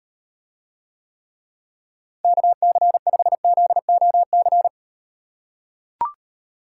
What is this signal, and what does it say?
Morse code, heard as a keyed tone, about 38 wpm, 700 Hz. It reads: KC5ZOC